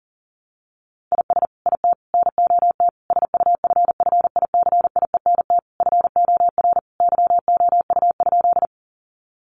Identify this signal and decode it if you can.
Morse code, heard as a keyed tone, about 40 wpm, 700 Hz. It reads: IS IT NOT SUFFICIENT FOR YOU?
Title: Sentences from Top 1400 Words 40wpm